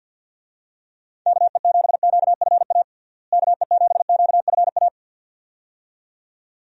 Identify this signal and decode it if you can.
Morse code, heard as a keyed tone, about 46 wpm, 700 Hz. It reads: KE7QFA KE7QFA
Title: USA Call Signs - 6 Characters - Repeated 2x - 50wpm